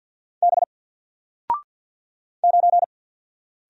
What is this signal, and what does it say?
Morse code, heard as a keyed tone, about 48 wpm, 700 Hz. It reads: B 9
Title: Single Letter-Number Alt 50wpm